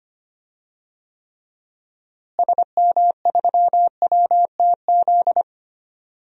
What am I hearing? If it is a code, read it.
Morse code, heard as a keyed tone, about 25 wpm, 700 Hz. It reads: SM3WTZ
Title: DXCC-100 Call Signs - 6 Characters - Repeated 2x - 25wpm